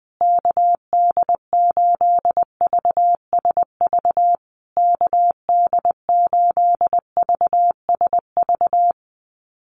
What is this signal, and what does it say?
Morse code, heard as a keyed tone, about 20 wpm, 700 Hz. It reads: KD8VSV KD8VSV
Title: USA Call Signs Repeated 3x 20wpm